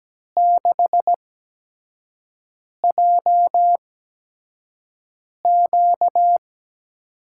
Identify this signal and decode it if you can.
Morse code, heard as a keyed tone, about 17 wpm, spaced as for 5 wpm, 700 Hz. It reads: 6JQ
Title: USA Call Signs - 5 Characters - 8x Character Spacing - 17wpm